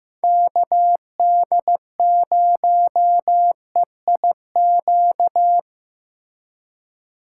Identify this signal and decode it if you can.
Morse code, heard as a keyed tone, about 18 wpm, 700 Hz. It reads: KD0EIQ
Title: Speed Racing - USA 2x3 Call Signs 15wpm